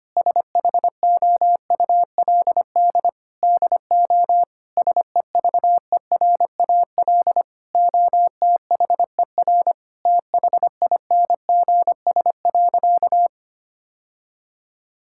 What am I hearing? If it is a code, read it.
Morse code, heard as a keyed tone, about 25 wpm, 700 Hz. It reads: SHOULD DO SEVERAL OTHER THINGS.
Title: Sentences from Top 1000 Words - Review Part 6 - 25wpm